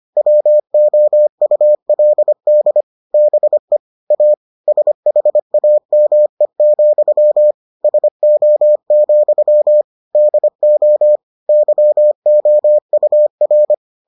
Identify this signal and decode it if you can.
Morse code, heard as a keyed tone, about 25 wpm, 600 Hz. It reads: WOULD BE A SHAME, SO, DO YOUR